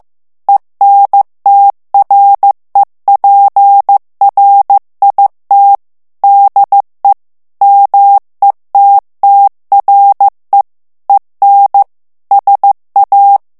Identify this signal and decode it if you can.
Morse code, heard as a keyed tone, about 15 wpm, 800 Hz. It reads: ENTREPRIT DE METTRE EN SA
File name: FRANCAIS TEXTE CLAIR 2 VITESSE 14 MOTS MINUTES